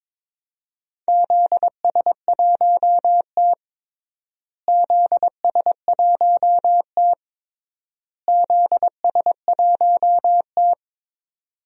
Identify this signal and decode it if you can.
Morse code, heard as a keyed tone, about 22 wpm, 700 Hz. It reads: ZS1T ZS1T ZS1T
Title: DXCC-100 Call Signs - 4 Characters - Repeated 3x - 22wpm